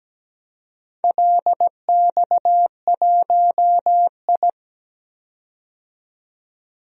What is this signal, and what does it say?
Morse code, heard as a keyed tone, about 17 wpm, 700 Hz. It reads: LX1I
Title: DXCC-100 Call Signs - 4 Characters - 17wpm